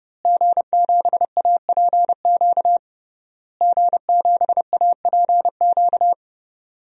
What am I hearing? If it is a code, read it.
Morse code, heard as a keyed tone, about 30 wpm, 700 Hz. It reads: G7APQ G7APQ
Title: DXCC-100 Call Signs - 5 Characters - Repeated 3x - 30wpm